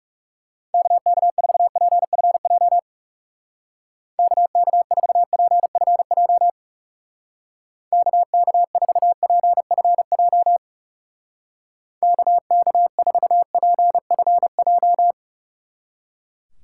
Morse code, heard as a keyed tone, about 38 wpm, 700 Hz. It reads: KK4PFJ KK4PFJ KK4PFJ KK4PFJ